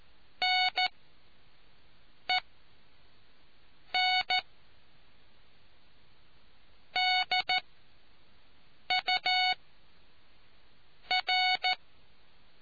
Morse code, heard as a keyed tone, about 14 wpm, spaced as for 4 wpm, 740 Hz. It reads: NENDUR